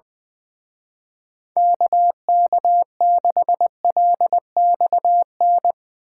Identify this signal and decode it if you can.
Morse code, heard as a keyed tone, about 22 wpm, 700 Hz. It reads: KK6LXN
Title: Speed Racing - USA 2x3 Call Signs 20wpm